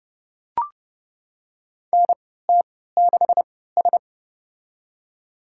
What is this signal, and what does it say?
Morse code, heard as a keyed tone, about 30 wpm, spaced as for 17 wpm, 700 Hz. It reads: NT6S
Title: USA Call Signs - 4 Characters - 3x Character Spacing - 30wpm